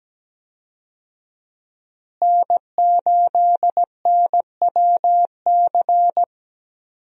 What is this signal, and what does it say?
Morse code, heard as a keyed tone, about 17 wpm, 700 Hz. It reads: N8NWC
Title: USA Call Signs - 5 Characters - Repeated 3x - 17wpm